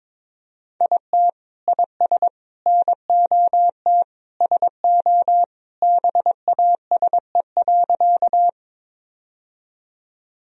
Morse code, heard as a keyed tone, about 22 wpm, 700 Hz. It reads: IT IS NOT SO BASE.
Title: Sentences from Top 1000 Words - Review Part 1 - 22wpm